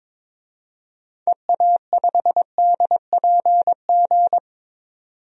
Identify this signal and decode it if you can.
Morse code, heard as a keyed tone, about 22 wpm, 700 Hz. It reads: EA5DPG
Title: DXCC-100 Call Signs - 6 Characters - 22wpm